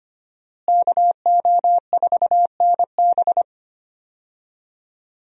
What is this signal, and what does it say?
Morse code, heard as a keyed tone, about 25 wpm, 700 Hz. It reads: KO4NB
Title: USA Call Signs - 5 Characters - Repeated 3x - 25wpm